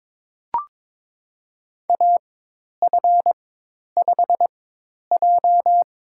Morse code, heard as a keyed tone, about 22 wpm, spaced as for 10 wpm, 700 Hz. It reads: AF5J
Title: USA Call Signs - 4x Character Spacing - 22wpm